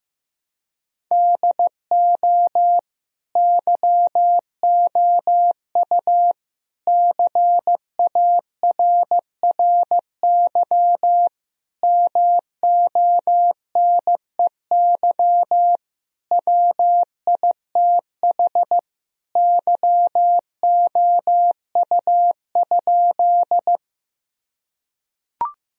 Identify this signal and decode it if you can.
Morse code, heard as a keyed tone, about 15 wpm, 700 Hz. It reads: DO YOU CARRY MONEY WITH YOU?
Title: Sentences from Top 1000 Words - Review Part 1 - 15wpm